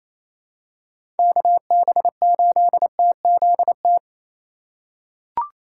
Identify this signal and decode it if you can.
Morse code, heard as a keyed tone, about 27 wpm, 700 Hz. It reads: KB8TZT
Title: USA Call Signs - 3x Character Spacing - 28wpm